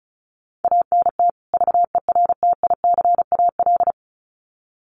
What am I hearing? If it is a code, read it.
Morse code, heard as a keyed tone, about 34 wpm, 700 Hz. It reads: ANT VERTICAL
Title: Common QSO Phrases 35wpm